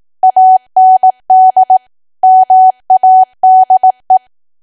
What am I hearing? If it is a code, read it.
Morse code, heard as a keyed tone, about 18 wpm, 750 Hz. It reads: AND MADE
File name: Archive / 18 WPM / 18WPM